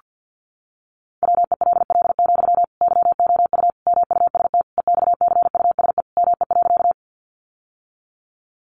Morse code, heard as a keyed tone, about 49 wpm, 700 Hz. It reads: WELL, YOU MUST EXCUSE ME.
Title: Sentences from Top 700 Words 50wpm